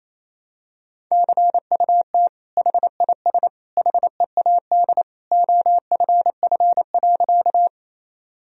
Morse code, heard as a keyed tone, about 28 wpm, 700 Hz. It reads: CUT HIS HEAD OFF.
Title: Sentences from Top 1000 Words - Review Part 1 - 28wpm